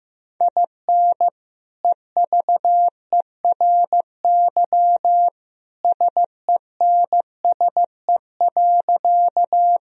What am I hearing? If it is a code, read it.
Morse code, heard as a keyed tone, about 15 wpm, 700 Hz. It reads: IN EVERY SENSE.
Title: Sentences from Top 1000 Words - Review Part 4 - 15wpm